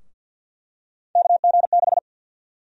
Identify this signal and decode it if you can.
Morse code, heard as a keyed tone, about 52 wpm, 700 Hz. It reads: KG6
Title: USA Call Sign Prefixes - 50wpm